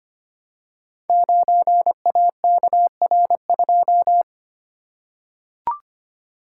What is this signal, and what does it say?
Morse code, heard as a keyed tone, about 25 wpm, 700 Hz. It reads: 9AKR2
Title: Code Groups - 5 Letter:Numbers - 25wpm